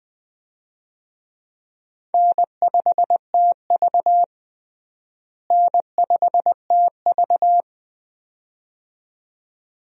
Morse code, heard as a keyed tone, about 20 wpm, 700 Hz. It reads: N5TV N5TV
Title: USA Call Signs - 4 Characters - Repeated 2x - 20wpm